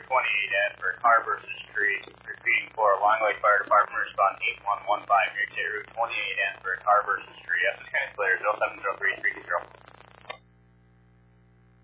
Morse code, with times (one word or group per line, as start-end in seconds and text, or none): none